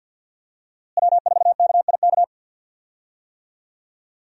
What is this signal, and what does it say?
Morse code, heard as a keyed tone, about 45 wpm, 700 Hz. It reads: W4KIK
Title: USA Call Signs - 5 Characters - Repeated 2x - 50wpm